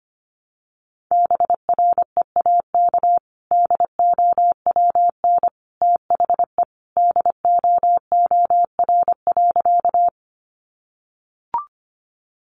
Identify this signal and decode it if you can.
Morse code, heard as a keyed tone, about 25 wpm, 700 Hz. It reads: BREAK DOWN THE DOOR.
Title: Sentences from Top 900 Words 25wpm